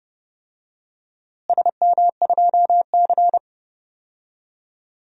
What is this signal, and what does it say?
Morse code, heard as a keyed tone, about 30 wpm, 700 Hz. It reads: SM2C
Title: DXCC-100 Call Signs - 4 Characters - Repeated 2x - 30wpm